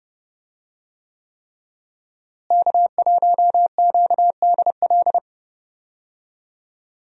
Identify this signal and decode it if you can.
Morse code, heard as a keyed tone, about 30 wpm, 700 Hz. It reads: K1QDL
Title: USA Call Signs - 5 Characters - Repeated 3x - 30wpm